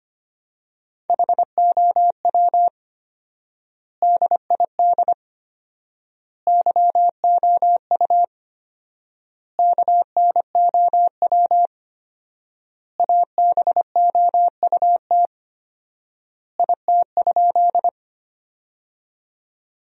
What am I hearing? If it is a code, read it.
Morse code, heard as a keyed tone, about 25 wpm, 700 Hz. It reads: HOW DID YOU KNOW ABOUT IT?
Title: 4x Word-Spacing - Sentences from Top 100 Words - 25wpm